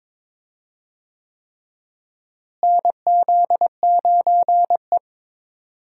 Morse code, heard as a keyed tone, about 22 wpm, 700 Hz. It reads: NZ9E
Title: USA Call Signs - 4 Characters - Repeated 3x - 22wpm